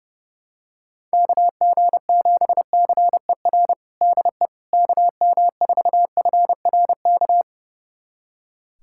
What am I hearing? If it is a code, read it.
Morse code, heard as a keyed tone, about 30 wpm, 700 Hz. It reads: KG7CER DE KM4FRK
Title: USA Callsign DE Callsign 30wpm